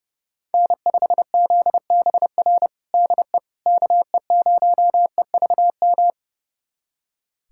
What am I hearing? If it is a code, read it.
Morse code, heard as a keyed tone, about 30 wpm, 700 Hz. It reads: N5ZBR DE KE0EVM
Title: USA Callsign DE Callsign 30wpm